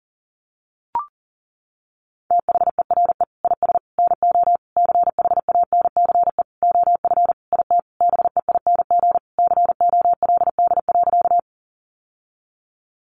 Morse code, heard as a keyed tone, about 40 wpm, 700 Hz. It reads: THERE IS NO CHANCE OF IT BEING COLD.
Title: Sentences from Top 800 Words 40wpm